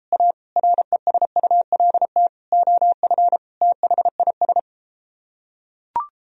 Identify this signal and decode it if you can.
Morse code, heard as a keyed tone, about 33 wpm, 700 Hz. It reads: A RESULT OF THIS